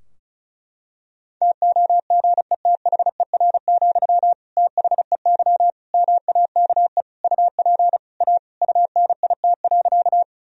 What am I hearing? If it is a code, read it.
Morse code, heard as a keyed tone, about 34 wpm, 700 Hz. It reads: TOGETHER, THEY MAKE UP A UNIT.
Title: Sentences from Top 1000 Words - Review Part 2 - 35wpm